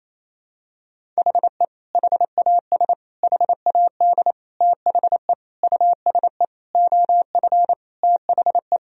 Morse code, heard as a keyed tone, about 27 wpm, 700 Hz. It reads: HE HAS HAD THE USE OF THE
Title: Sentences from Top 1000 Words - Review Part 1 - 28wpm